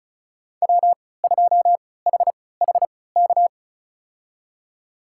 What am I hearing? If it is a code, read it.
Morse code, heard as a keyed tone, about 36 wpm, spaced as for 20 wpm, 690 Hz. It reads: W2HHK
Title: USA Call Signs - 5 Characters - 3x Character Spacing - 35wpm